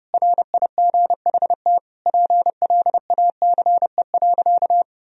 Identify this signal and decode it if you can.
Morse code, heard as a keyed tone, about 32 wpm, 700 Hz. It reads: RIGHT PLACE.
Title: Sentences from Top 1000 Words - Review Part 3 - 30wpm